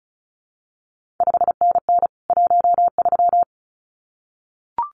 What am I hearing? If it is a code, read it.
Morse code, heard as a keyed tone, about 36 wpm, 700 Hz. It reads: 5NN 13
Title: CQ WW DX Contest Exchange Encore - 35wpm